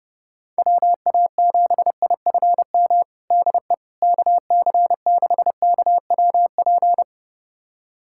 Morse code, heard as a keyed tone, about 30 wpm, 700 Hz. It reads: WA7IFM DE KC6KWP
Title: USA Callsign DE Callsign - 6 Characters - 30wpm